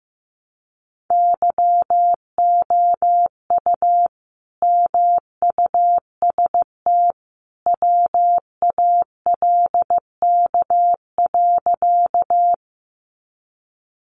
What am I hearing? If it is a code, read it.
Morse code, heard as a keyed tone, about 15 wpm, 700 Hz. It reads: YOU MUST WALK.
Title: Sentences from Top 300 Words 15wpm